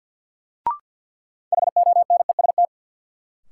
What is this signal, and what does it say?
Morse code, heard as a keyed tone, about 53 wpm, 700 Hz. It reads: HONEST
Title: Sets of 5 Words Encore - New Word Drill - 50wpm